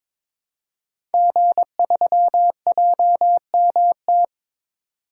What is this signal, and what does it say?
Morse code, heard as a keyed tone, about 22 wpm, 700 Hz. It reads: G3JMT
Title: Code Groups - 5 Letter:Numbers - 22wpm